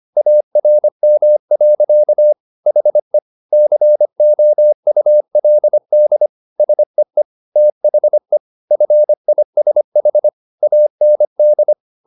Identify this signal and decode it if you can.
Morse code, heard as a keyed tone, about 25 wpm, 600 Hz. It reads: ARM. HE COULD SEE THE FISH AND